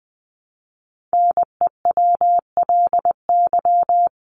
Morse code, heard as a keyed tone, about 20 wpm, 700 Hz. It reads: NEWLY
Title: LY Suffix - Top 100 Words with Base Words 20wpm